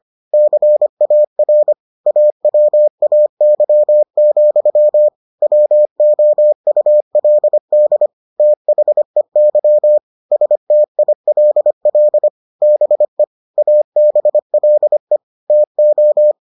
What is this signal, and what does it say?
Morse code, heard as a keyed tone, about 25 wpm, 600 Hz. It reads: CAR AWAY, WOULD THEY STILL BE ABLE TO